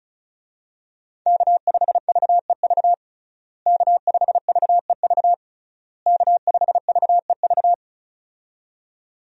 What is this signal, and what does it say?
Morse code, heard as a keyed tone, about 34 wpm, 700 Hz. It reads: K5VEV K5VEV K5VEV
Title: USA Call Signs - 5 Characters - Repeated 3x - 35wpm